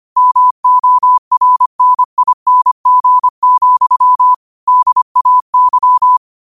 Morse code, heard as a keyed tone, about 25 wpm, 1000 Hz. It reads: MORNING, DAY
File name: qotdfiles / 25wpm